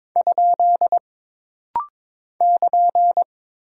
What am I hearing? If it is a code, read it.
Morse code, heard as a keyed tone, about 22 wpm, 700 Hz. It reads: ? <KN>
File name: practice / 22wpm / Single Punctuation-Prosign - Rapid-Fire - 22wpm